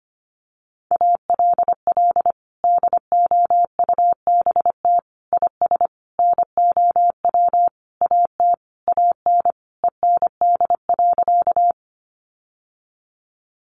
Morse code, heard as a keyed tone, about 25 wpm, 700 Hz. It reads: ALL DOUBT IS NOW AT AN END.
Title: Sentences from Top 1500 Words 25wpm